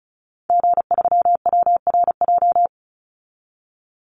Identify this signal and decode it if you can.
Morse code, heard as a keyed tone, about 34 wpm, 700 Hz. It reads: G3WRJ
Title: Call Signs Repeated 3x - 35wpm